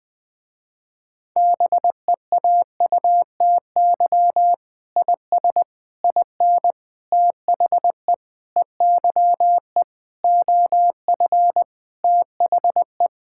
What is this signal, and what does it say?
Morse code, heard as a keyed tone, about 20 wpm, 700 Hz. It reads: BEAUTY IS IN THE EYE OF THE
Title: Sentences with 8-Word Phrases - Spoken Beginning - 20wpm